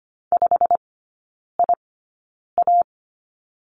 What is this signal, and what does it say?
Morse code, heard as a keyed tone, about 25 wpm, spaced as for 9 wpm, 700 Hz. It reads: 5IA
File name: Lesson 17e - Letter u - Three Characters 25wpm